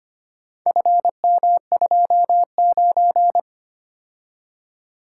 Morse code, heard as a keyed tone, about 25 wpm, 700 Hz. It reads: FM29
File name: practice / 25wpm / Four Letter-Number Combinations - 25wpm